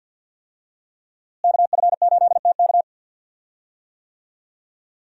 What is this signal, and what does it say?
Morse code, heard as a keyed tone, about 52 wpm, 700 Hz. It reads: KF8TK
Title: USA Call Signs - 5 Characters - Repeated 3x - 50wpm